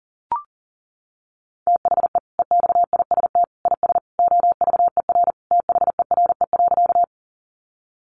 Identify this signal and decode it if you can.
Morse code, heard as a keyed tone, about 40 wpm, 700 Hz. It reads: THE EXIST IS OVER THERE.
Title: Sentences from Top 1100 Words 40wpm